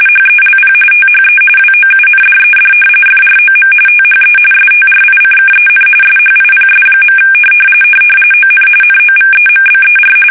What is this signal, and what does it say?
RTTY, about 45.45 baud. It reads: HELP THE WOUNDED FROM THE STRICKEN SHIPS. ON NOVEMBER
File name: fsk850 / battleshipnj